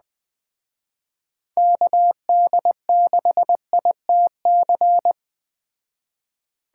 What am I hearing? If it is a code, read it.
Morse code, heard as a keyed tone, about 21 wpm, 700 Hz. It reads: KD6ITC
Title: Speed Racing - USA Call Signs 17wpm